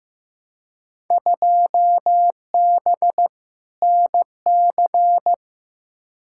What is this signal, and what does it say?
Morse code, heard as a keyed tone, about 15 wpm, 700 Hz. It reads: 2B NC
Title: Field Day Exchanges Encore - 15wpm